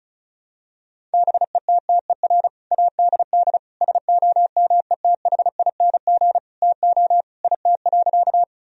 Morse code, heard as a keyed tone, about 36 wpm, 700 Hz. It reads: BETTER ADD SOMETHING TO IT.
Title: Sentences from Top 1000 Words - Review Part 1 - 35wpm